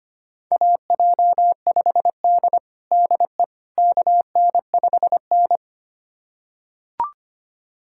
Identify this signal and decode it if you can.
Morse code, heard as a keyed tone, about 25 wpm, 700 Hz. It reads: AJ5D DE KN5N
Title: USA Callsign DE Callsign - 4 Characters - 25wpm